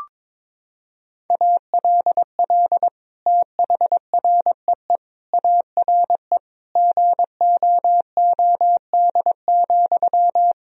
Morse code, heard as a keyed tone, about 22 wpm, 700 Hz. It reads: ALL THREE ARE GOOD,
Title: Sentences from Top 1000 Words - Review Part 4 - 22wpm